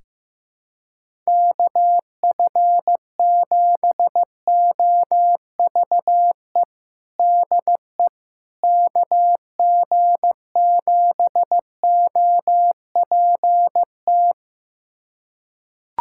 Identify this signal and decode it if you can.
Morse code, heard as a keyed tone, about 16 wpm, 700 Hz. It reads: KF7OVE DE KG7OPT
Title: USA Callsign DE Callsign - 6 Characters - 15wpm